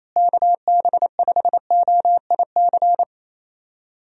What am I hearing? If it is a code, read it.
Morse code, heard as a keyed tone, about 28 wpm, 700 Hz. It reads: KB5OIC
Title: USA Call Signs - 6 Characters - Repeated 2x - 28wpm